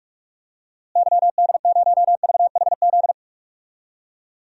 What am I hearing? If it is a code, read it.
Morse code, heard as a keyed tone, about 46 wpm, 700 Hz. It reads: YD0VHZ
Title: DXCC-100 Call Signs - 6 Characters - Repeated 2x - 45wpm